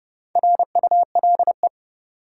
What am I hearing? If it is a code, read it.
Morse code, heard as a keyed tone, about 30 wpm, 700 Hz. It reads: RULE
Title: Sentences from Top 400 Words - New Word Drill - 30wpm